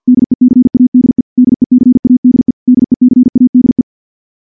Morse code, heard as a keyed tone, about 50 wpm, 250 Hz. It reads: DECADE DECADE DECADE
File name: morse / morse-training-50WPM